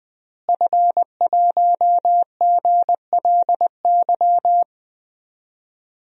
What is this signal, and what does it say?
Morse code, heard as a keyed tone, about 20 wpm, 700 Hz. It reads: F1GLY
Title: DXCC-100 Call Signs - 5 Characters - Repeated 2x - 20wpm